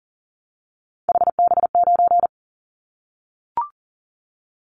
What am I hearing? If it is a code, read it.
Morse code, heard as a keyed tone, about 40 wpm, 700 Hz. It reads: HB9
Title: Callsign Prefixes for Top 25 DXCC - 40wpm